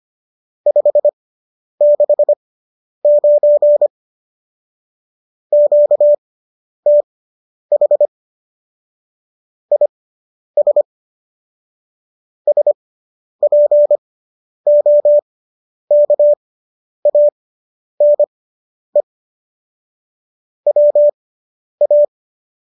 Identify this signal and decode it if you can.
Morse code, heard as a keyed tone, about 25 wpm, spaced as for 10 wpm, 590 Hz. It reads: 569 QTH IS SPOKANE WA